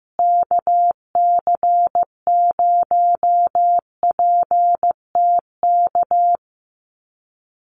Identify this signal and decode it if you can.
Morse code, heard as a keyed tone, about 15 wpm, 700 Hz. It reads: KC0PTK
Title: USA Call Signs Repeated 2x 15wpm